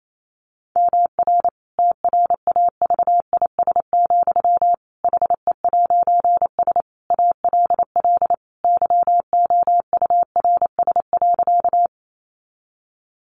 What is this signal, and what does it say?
Morse code, heard as a keyed tone, about 28 wpm, 700 Hz. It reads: MR TRAVIS, HE'S ALL YOURS.
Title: A Sound Of Thunder 28wpm